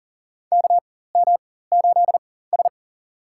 Morse code, heard as a keyed tone, about 40 wpm, spaced as for 19 wpm, 700 Hz. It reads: KM8S
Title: USA Call Signs - 4 Characters - 4x Character Spacing - 40wpm